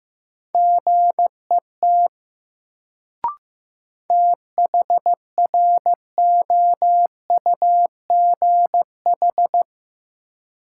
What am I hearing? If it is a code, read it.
Morse code, heard as a keyed tone, about 15 wpm, 700 Hz. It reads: GET THROUGH
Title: Sets of 2 Words - New Word Drill - 15wpm